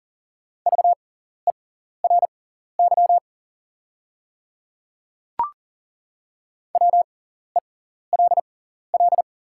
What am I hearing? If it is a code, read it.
Morse code, heard as a keyed tone, about 40 wpm, spaced as for 14 wpm, 690 Hz. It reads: VERY WELL
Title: 6x Character Spacing - Top 200 Words - 40wpm